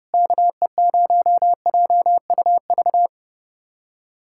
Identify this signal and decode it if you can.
Morse code, heard as a keyed tone, about 30 wpm, 700 Hz. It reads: KE0JUV
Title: USA Callsign DE Callsign 30wpm